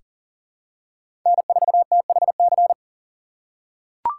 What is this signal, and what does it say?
Morse code, heard as a keyed tone, about 40 wpm, 700 Hz. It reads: N4THC
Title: USA Call Signs - 6x Character Spacing - 40wpm